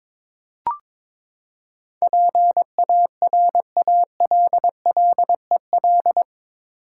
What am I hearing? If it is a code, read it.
Morse code, heard as a keyed tone, about 22 wpm, 700 Hz. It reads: PARALLEL
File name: practice / 22wpm / Eight Letter Words - 22wpm